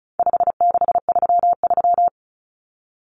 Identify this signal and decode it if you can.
Morse code, heard as a keyed tone, about 34 wpm, 700 Hz. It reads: 5633
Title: Numbers - 4-Digit 35wpm